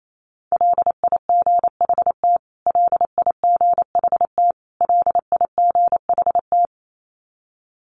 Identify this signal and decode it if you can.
Morse code, heard as a keyed tone, about 28 wpm, 700 Hz. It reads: LIGHT LIGHT LIGHT
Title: Top 300 Words Repeated 28wpm